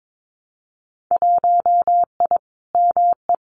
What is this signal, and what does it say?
Morse code, heard as a keyed tone, about 22 wpm, 700 Hz. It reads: 1I ME
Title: Winter Field Day Exchanges 22wpm